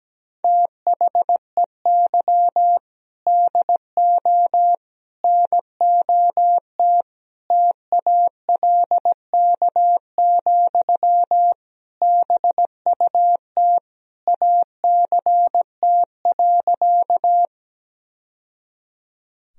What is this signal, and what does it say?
Morse code, heard as a keyed tone, about 17 wpm, 700 Hz. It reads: THEY DO NOT TALK, BUT ACT.
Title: Sentences from Top 1000 Words - Review Part 1 - 17wpm